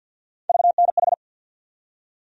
Morse code, heard as a keyed tone, about 47 wpm, 700 Hz. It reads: VNH
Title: USA Call Sign Postfixes - 3 Letters 50wpm